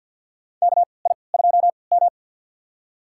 Morse code, heard as a keyed tone, about 51 wpm, spaced as for 28 wpm, 700 Hz. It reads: KI2M
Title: USA Call Signs - 4 Characters - 3x Character Spacing - 50wpm